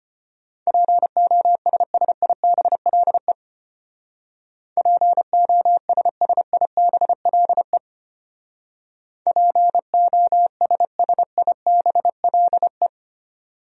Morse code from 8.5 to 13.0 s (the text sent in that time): POSSIBLE